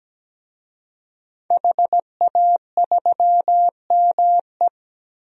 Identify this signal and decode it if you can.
Morse code, heard as a keyed tone, about 17 wpm, 700 Hz. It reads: HA3ME
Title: DXCC-100 Call Signs - 5 Characters - 17wpm